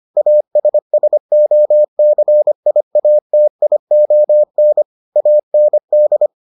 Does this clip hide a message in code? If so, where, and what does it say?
Yes: Morse code from 0.2 to 6.3 s, heard as a keyed tone, about 25 wpm, 600 Hz. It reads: ASSOCIATION AND